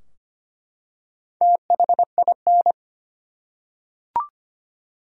Morse code, heard as a keyed tone, about 25 wpm, 700 Hz. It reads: THIN
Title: Four Letter Words - 25wpm